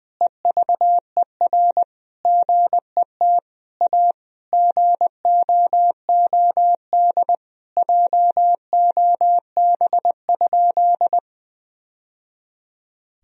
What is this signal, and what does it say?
Morse code, heard as a keyed tone, about 20 wpm, 700 Hz. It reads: EVER GET A GOOD JOB?
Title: Sentences from Top 1000 Words - Review Part 3 - 20wpm